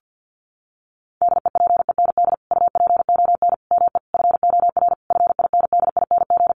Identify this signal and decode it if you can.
Morse code, heard as a keyed tone, about 50 wpm, 700 Hz. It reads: DEPEND UPON ME FOR FINDING
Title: Sentences from Top 1300 Words 50wpm